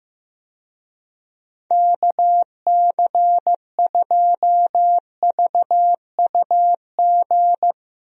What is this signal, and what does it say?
Morse code, heard as a keyed tone, about 15 wpm, 700 Hz. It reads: KC2VUG
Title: USA Call Signs - 6 Characters - Repeated 3x - 15wpm